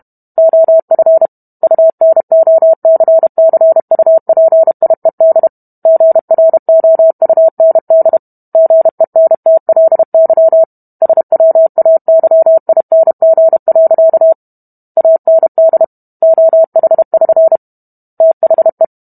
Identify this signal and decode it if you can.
Morse code, heard as a keyed tone, about 32 wpm, 650 Hz. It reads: OF UNOCCUPIED GROUND GENTLY SWAYING. AND OH<SN> THE